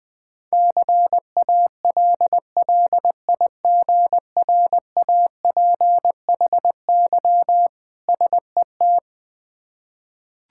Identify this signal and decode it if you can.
Morse code, heard as a keyed tone, about 21 wpm, 700 Hz. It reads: CALLIGRAPHY SET
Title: Difficult Words with Hints - 20wpm